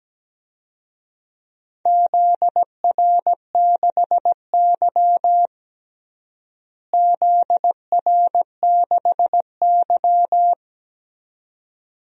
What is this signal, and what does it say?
Morse code, heard as a keyed tone, about 17 wpm, 700 Hz. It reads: ZR6Y ZR6Y
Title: DXCC-100 Call Signs - 4 Characters - Repeated 2x - 17wpm